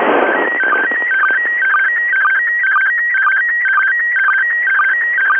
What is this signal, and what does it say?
SSTV (slow-scan television), PD 120.
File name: dc9dd / isssstv